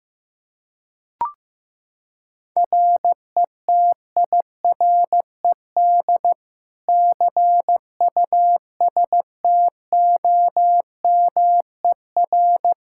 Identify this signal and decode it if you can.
Morse code, heard as a keyed tone, about 15 wpm, 700 Hz. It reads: RETIRED CUSTOMER
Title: Practice QSO - Exchange 3 - More Info - 15wpm